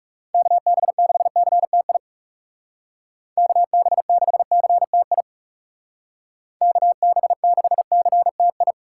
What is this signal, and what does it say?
Morse code, heard as a keyed tone, about 40 wpm, 700 Hz. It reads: KB6CTI KB6CTI KB6CTI